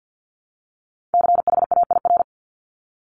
Morse code, heard as a keyed tone, about 50 wpm, 700 Hz. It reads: CHAIR
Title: Top 1751-2000 Words 50wpm